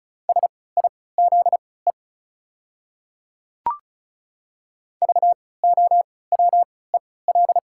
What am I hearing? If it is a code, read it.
Morse code, heard as a keyed tone, about 34 wpm, spaced as for 20 wpm, 700 Hz. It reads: SIZE VOWEL
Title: 3x Character Spacing - Top 500 Words - 35wpm